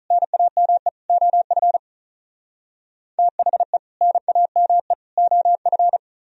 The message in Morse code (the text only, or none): NAME OF THE NAME OF